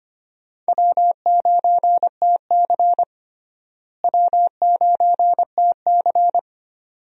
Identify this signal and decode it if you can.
Morse code, heard as a keyed tone, about 25 wpm, 700 Hz. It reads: W9TC W9TC
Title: USA Call Signs - 4 Characters - Repeated 2x - 25wpm